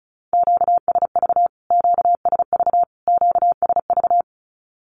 Morse code, heard as a keyed tone, about 34 wpm, 700 Hz. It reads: QSV QSV QSV
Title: Q-Codes Repeated 3x 35wpm